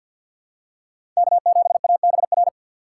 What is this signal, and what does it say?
Morse code, heard as a keyed tone, about 53 wpm, 700 Hz. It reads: K7ABR
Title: USA Call Signs - 5 Characters - 2x Character Spacing - 50wpm